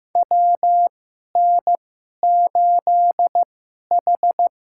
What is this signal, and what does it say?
Morse code, heard as a keyed tone, about 15 wpm, spaced as for 11 wpm, 700 Hz. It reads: WN8H
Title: USA Call Signs - 4 Characters - 2x Character Spacing - 15wpm